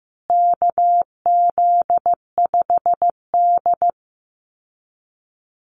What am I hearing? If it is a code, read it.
Morse code, heard as a keyed tone, about 15 wpm, 700 Hz. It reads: KZ5D
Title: Call Signs Repeated 2x - 15wpm